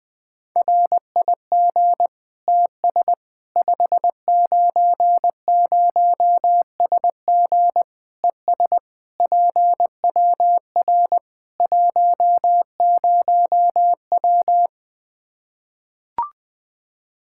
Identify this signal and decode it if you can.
Morse code, heard as a keyed tone, about 20 wpm, 700 Hz. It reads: RIG TS 590SG ES PWR 10W
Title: Practice QSO - Exchange 2 - Info - 20wpm